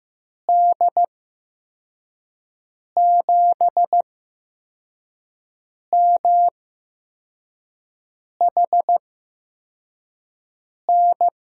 Morse code, heard as a keyed tone, about 15 wpm, spaced as for 4 wpm, 700 Hz. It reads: D7MHN
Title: USA Call Signs - 6 Characters - 8x Character Spacing - 15wpm